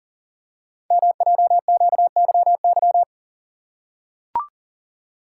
Morse code, heard as a keyed tone, about 42 wpm, 700 Hz. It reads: MJQYY
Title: Code Groups - 5 Letters - 40wpm